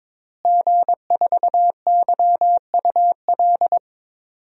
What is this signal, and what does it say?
Morse code, heard as a keyed tone, about 22 wpm, 700 Hz. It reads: G4YUL
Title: DXCC-100 Call Signs - 5 Characters - Repeated 3x - 22wpm